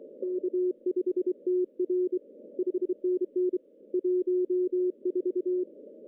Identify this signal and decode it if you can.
Morse code, heard as a keyed tone, about 24 wpm, 360 Hz. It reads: K5TR 5NN 14
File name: CW / cqww-cw-k5tr